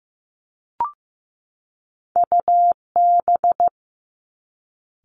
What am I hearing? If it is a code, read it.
Morse code, heard as a keyed tone, about 15 wpm, 700 Hz. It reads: UB
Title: Top Two Letter Combinations in Words 15wpm